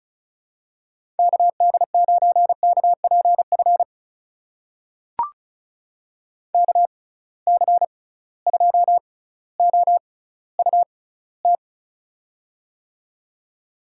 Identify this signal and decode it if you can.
Morse code, heard as a keyed tone, about 36 wpm, spaced as for 18 wpm, 690 Hz. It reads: KD9KPF KC2OUT
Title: USA Call Signs - 6 Characters - 6x Character Spacing - 35wpm